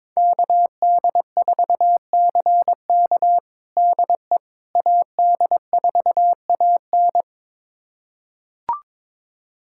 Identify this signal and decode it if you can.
Morse code, heard as a keyed tone, about 22 wpm, 700 Hz. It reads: KD4CK DE AD4AN
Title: USA Callsign DE Callsign - 5 Characters - 22wpm